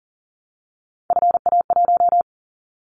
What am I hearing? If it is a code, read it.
Morse code, heard as a keyed tone, about 41 wpm, 700 Hz. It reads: FA1
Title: Callsign Prefixes for Top 50 DXCC - 40wpm